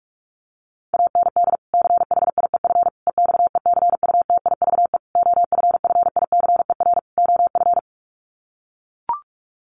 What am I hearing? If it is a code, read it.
Morse code, heard as a keyed tone, about 44 wpm, 700 Hz. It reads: AND CHIEF EXECUTIVE OFFICER OF
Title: Sets of 5 Words Encore2 45wpm